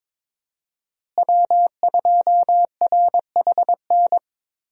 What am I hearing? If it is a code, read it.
Morse code, heard as a keyed tone, about 22 wpm, 700 Hz. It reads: W2RHN
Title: USA Call Signs - 5 Characters - Repeated 2x - 22wpm